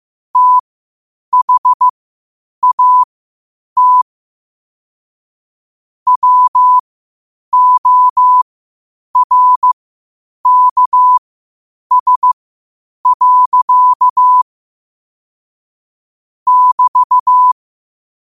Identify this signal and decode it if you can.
Morse code, heard as a keyed tone, about 15 wpm, spaced as for 8 wpm, 1000 Hz. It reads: THAT WORKS. <BT>
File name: qotdfiles / 8wpm